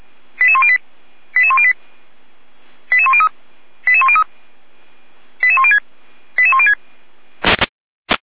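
Five-tone selective call (ZVEI), heard as quick runs of five tones, tones of about 70 ms. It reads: 88108 88108 88103 88103 88106 88106
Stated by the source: ZVEI five-tone selective call 88327 received by SDR.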